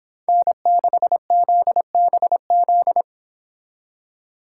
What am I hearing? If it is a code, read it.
Morse code, heard as a keyed tone, about 26 wpm, 700 Hz. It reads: N6ZBZ
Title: Speed Racing - USA 1x3 Call Signs 22wpm